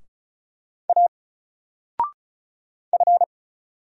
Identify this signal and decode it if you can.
Morse code, heard as a keyed tone, about 33 wpm, 700 Hz. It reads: A F
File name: Single-Letter-Number-35wpm